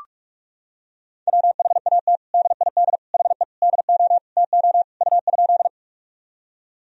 Morse code, heard as a keyed tone, about 46 wpm, 700 Hz. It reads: WHAT DID HE DO TO U?
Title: Sentences from Top 100 Words and Abbreviations 45wpm